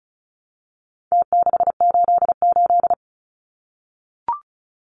Morse code, heard as a keyed tone, about 34 wpm, 700 Hz. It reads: T688
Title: Numbers - Contest Style - 4-Digit 35wpm